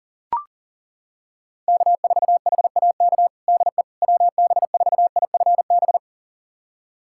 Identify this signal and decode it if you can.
Morse code, heard as a keyed tone, about 40 wpm, 700 Hz. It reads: K4HAK DE WB4IFB
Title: USA Callsign DE Callsign 40wpm